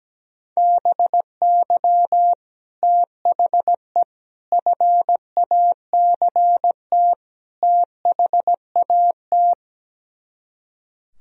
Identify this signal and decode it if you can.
Morse code, heard as a keyed tone, about 17 wpm, 700 Hz. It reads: BY THE FACT THAT
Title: Speed-Racing - Sets of 4 Words 17wpm